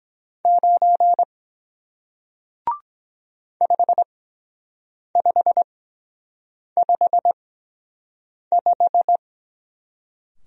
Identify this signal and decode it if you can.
Morse code, heard as a keyed tone, about 23 wpm, 700 Hz. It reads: 9 5 5 5 5